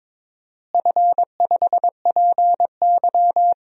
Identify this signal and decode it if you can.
Morse code, heard as a keyed tone, about 22 wpm, 700 Hz. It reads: F5PY
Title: DXCC-100 Call Signs - 4 Characters - Repeated 3x - 22wpm